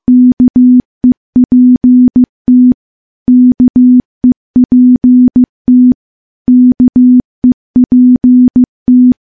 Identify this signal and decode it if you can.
Morse code, heard as a keyed tone, about 15 wpm, 250 Hz. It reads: KEPT KEPT KEPT
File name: morse / morse-training-15WPM